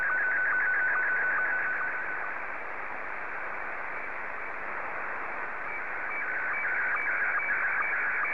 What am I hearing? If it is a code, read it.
SSTV (slow-scan television), Scottie 1.